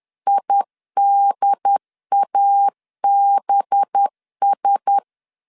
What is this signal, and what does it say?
Morse code, heard as a keyed tone, about 10 wpm, 800 Hz. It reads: IDABS
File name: LEZIONI / 7 GF9Z / 7 GF9Z 10wpm 2611